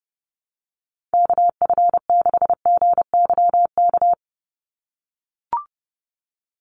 Morse code, heard as a keyed tone, about 30 wpm, 700 Hz. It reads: KF6GYK
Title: USA Call Signs 30wpm